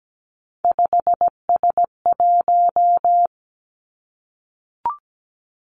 Morse code, heard as a keyed tone, about 17 wpm, 700 Hz. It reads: 5S1
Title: Three Challenging Letter-Number Combinations 17wpm